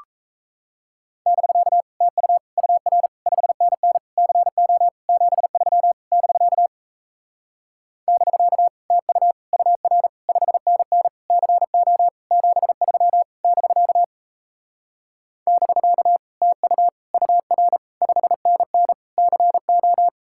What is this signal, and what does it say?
Morse code, heard as a keyed tone, about 37 wpm, 700 Hz. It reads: <BK> TU UR 5NN CO 73 <BK> <BK> TU UR 5NN CO 73 <BK> <BK> TU UR 5NN CO